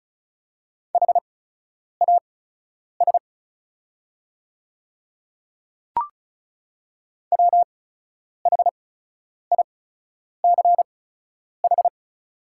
Morse code, heard as a keyed tone, about 36 wpm, spaced as for 10 wpm, 700 Hz. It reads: HAS WHICH
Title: 8x Character Spacing - Top 100 Words - 35wpm